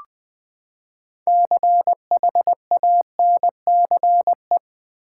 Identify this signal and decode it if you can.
Morse code, heard as a keyed tone, about 20 wpm, 700 Hz. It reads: CHANCE
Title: Six Letter Words - 20wpm